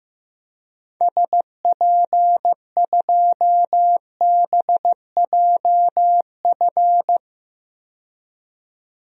Morse code, heard as a keyed tone, about 15 wpm, 700 Hz. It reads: SP2BJF
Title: DXCC-100 Call Signs - 6 Characters - 15wpm